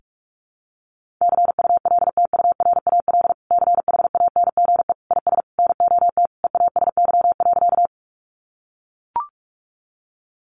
Morse code, heard as a keyed tone, about 44 wpm, 700 Hz. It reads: CULTURAL CHANGE IS NOT EASY.
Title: Sentences from Top 1500 Words 45wpm